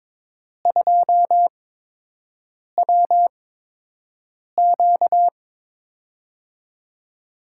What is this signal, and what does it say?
Morse code, heard as a keyed tone, about 22 wpm, spaced as for 6 wpm, 700 Hz. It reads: 2WQ
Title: USA Call Signs - 4 Characters - 8x Character Spacing - 22wpm